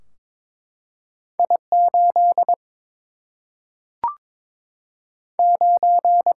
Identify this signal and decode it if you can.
Morse code, heard as a keyed tone, about 22 wpm, 700 Hz. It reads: I 8 9
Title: Single-Letter Single-Number 22wpm